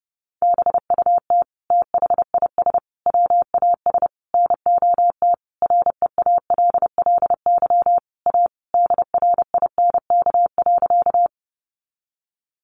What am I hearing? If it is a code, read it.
Morse code, heard as a keyed tone, about 30 wpm, 700 Hz. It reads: BUT THIS WAS NOT REALLY A DRINK.
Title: Sentences from Top 1500 Words 30wpm